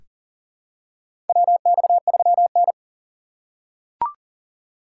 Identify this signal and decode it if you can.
Morse code, heard as a keyed tone, about 40 wpm, 700 Hz. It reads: WX3N
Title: USA Call Signs - 4 Characters - 2x Character Spacing - 40wpm